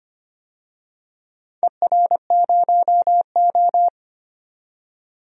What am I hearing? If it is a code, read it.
Morse code, heard as a keyed tone, about 25 wpm, 700 Hz. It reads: ER0O
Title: DXCC-100 Call Signs - 4 Characters - 25wpm